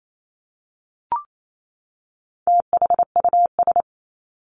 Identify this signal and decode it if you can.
Morse code, heard as a keyed tone, about 28 wpm, 700 Hz. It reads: THUS
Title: Top 751-1000 Words 28wpm